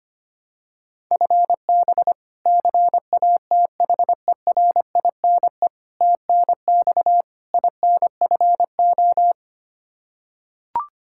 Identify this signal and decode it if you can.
Morse code, heard as a keyed tone, about 25 wpm, 700 Hz. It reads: FB CATHERINE TNX INFO
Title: K1USN SST QSO Practice 25wpm